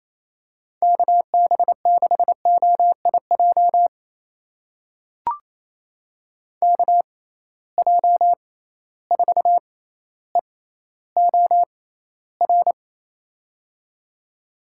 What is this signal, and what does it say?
Morse code, heard as a keyed tone, about 28 wpm, spaced as for 14 wpm, 700 Hz. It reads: KB6OIJ KJ4EOR
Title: USA Call Signs - 6 Characters - 6x Character Spacing - 28wpm